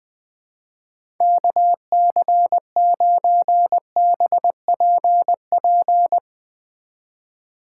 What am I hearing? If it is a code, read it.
Morse code, heard as a keyed tone, about 21 wpm, 700 Hz. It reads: KC9BPP
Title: USA Call Signs - 6 Characters - 4x Character Spacing - 20wpm